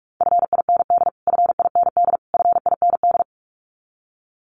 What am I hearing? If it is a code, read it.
Morse code, heard as a keyed tone, about 44 wpm, 700 Hz. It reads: FIND FIND FIND
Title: Top 200 Words Repeated 45wpm